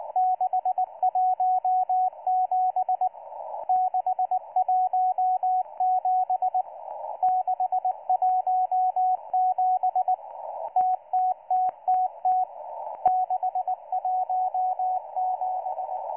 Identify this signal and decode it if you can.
Morse code, heard as a keyed tone, about 19 wpm, 740 Hz, with radio noise. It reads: 617 617 617 TTTTT 617